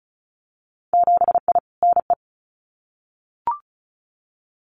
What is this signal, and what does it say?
Morse code, heard as a keyed tone, about 34 wpm, 700 Hz. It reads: 7I NE
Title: Winter Field Day Exchanges Encore 35wpm